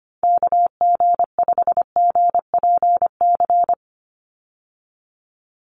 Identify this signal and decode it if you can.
Morse code, heard as a keyed tone, about 25 wpm, 700 Hz. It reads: KG5GPC
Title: USA Call Signs 25wpm